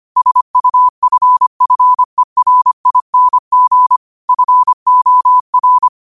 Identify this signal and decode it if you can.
Morse code, heard as a keyed tone, about 25 wpm, 1000 Hz. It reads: SUFFERING FOR
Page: New quotes every day in morse code at 25 Words per minute.